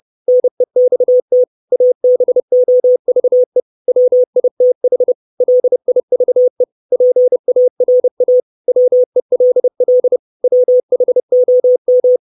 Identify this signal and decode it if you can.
Morse code, heard as a keyed tone, about 30 wpm, 500 Hz. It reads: NEXT ABOVE WITH LIVE PARA WELL WHOM
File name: 632 Words 30WPM